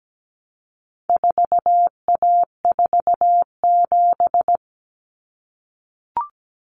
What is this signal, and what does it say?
Morse code, heard as a keyed tone, about 17 wpm, 700 Hz. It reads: 4A47
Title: Common Cut Numbers - 4-Digit 17wpm